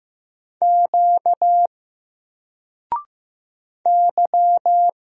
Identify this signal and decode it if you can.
Morse code, heard as a keyed tone, about 15 wpm, 700 Hz. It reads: Q Y
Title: Single Letter-Number 15wpm